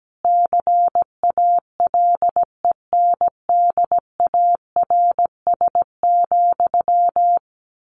Morse code, heard as a keyed tone, about 17 wpm, 700 Hz. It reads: CALENDARS,
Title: A Sound Of Thunder 17wpm